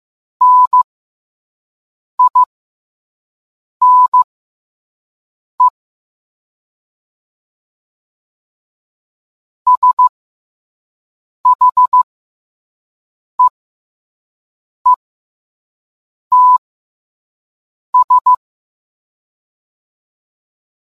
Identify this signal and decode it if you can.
Morse code, heard as a keyed tone, about 15 wpm, spaced as for 5 wpm, 1000 Hz. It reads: NINE SHEETS